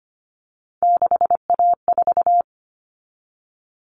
Morse code, heard as a keyed tone, about 25 wpm, 700 Hz. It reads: 6A4
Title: Common Cut Numbers - 3-Digit 25wpm